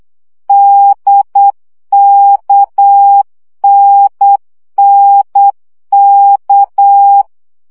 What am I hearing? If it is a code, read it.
Morse code, heard as a keyed tone, about 8 wpm, 800 Hz. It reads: DKNNK
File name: CW gyakorlas n d k 7 baud